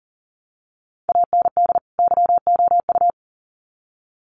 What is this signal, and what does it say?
Morse code, heard as a keyed tone, about 40 wpm, 700 Hz. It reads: AND YOU
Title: Sets of 2 Words 40wpm